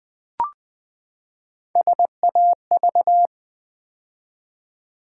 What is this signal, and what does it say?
Morse code, heard as a keyed tone, about 20 wpm, 700 Hz. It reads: SAV
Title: Build A Word - 3 Letters At A Time - 6-Letter Words - 20wpm